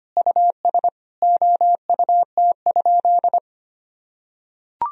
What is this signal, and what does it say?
Morse code, heard as a keyed tone, about 25 wpm, 700 Hz. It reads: US OUT?
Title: Sentences from Top 1000 Words - Review Part 3 - 25wpm